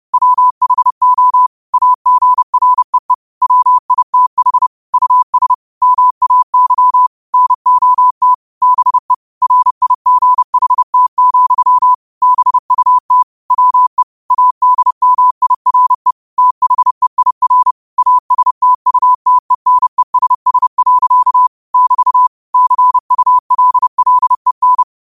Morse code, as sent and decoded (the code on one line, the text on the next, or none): .-- .... --- / .- --. .-. . / .-- .. - .... / ..- ... / -- .- -.-- / -. --- - / -... / .-. .. --. .... - --..-- / -... ..- - / .-- . / .- -.. -- .. .-. / - .... .. .-. / .- ... - ..- - . -. ... ... .-.-.- / -...- / -.-. ..- .-.. .-.. -.
WHO AGREE WITH US MAY NOT BE RIGHT, BUT WE ADMIRE THEIR ASTUTENESS. <BT> CULLEN